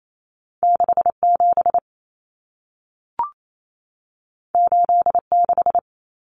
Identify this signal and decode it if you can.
Morse code, heard as a keyed tone, about 28 wpm, 700 Hz. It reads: 67 86
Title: Numbers - 2-Digit 28wpm